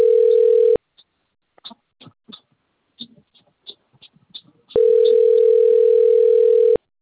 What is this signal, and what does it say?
Ringback tone (North America), 440 + 480 Hz.